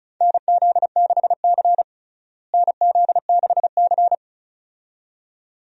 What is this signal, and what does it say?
Morse code, heard as a keyed tone, about 34 wpm, 700 Hz. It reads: NZ6C NZ6C
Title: USA Call Signs - 4 Characters - Repeated 2x - 35wpm